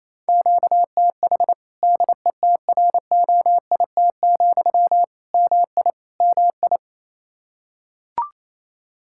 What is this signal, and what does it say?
Morse code, heard as a keyed tone, about 28 wpm, 700 Hz. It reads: QTH DETROIT, MI MI
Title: Practice QSO - Exchange 1 - Beginning - 28wpm